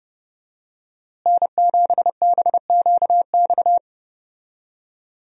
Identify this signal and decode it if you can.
Morse code, heard as a keyed tone, about 30 wpm, 700 Hz. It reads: N7BQX
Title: USA Call Signs - 5 Characters - Repeated 3x - 30wpm